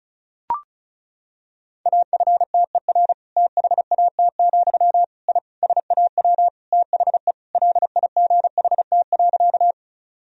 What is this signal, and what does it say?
Morse code, heard as a keyed tone, about 36 wpm, 700 Hz. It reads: AFTER THAT, I SAW THE LIGHT.
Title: Sentences from Top 1000 Words - Review Part 1 - 35wpm